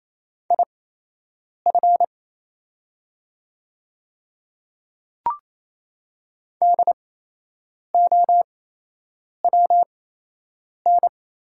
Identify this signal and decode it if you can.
Morse code, heard as a keyed tone, about 28 wpm, spaced as for 8 wpm, 700 Hz. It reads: IF DOWN
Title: 8x Character Spacing - Top 100 Words - 28wpm